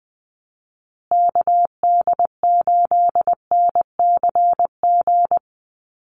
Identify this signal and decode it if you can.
Morse code, heard as a keyed tone, about 20 wpm, 700 Hz. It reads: KD8NCG
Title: USA Call Signs Repeated 3x 20wpm